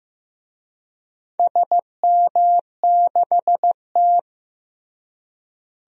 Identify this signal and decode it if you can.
Morse code, heard as a keyed tone, about 15 wpm, 700 Hz. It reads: SM6T
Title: DXCC-100 Call Signs - 4 Characters - Repeated 3x - 15wpm